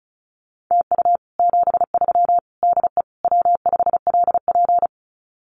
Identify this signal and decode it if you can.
Morse code, heard as a keyed tone, about 34 wpm, 700 Hz. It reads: TU 73 DE W5LP
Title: Common QSO Phrases Encore 35wpm